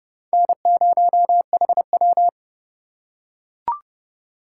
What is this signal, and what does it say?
Morse code, heard as a keyed tone, about 28 wpm, 700 Hz. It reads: N0HW